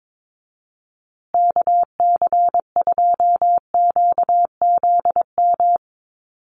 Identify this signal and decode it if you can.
Morse code, heard as a keyed tone, about 22 wpm, 700 Hz. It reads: KC2QZM
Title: USA Call Signs Repeated 2x 22wpm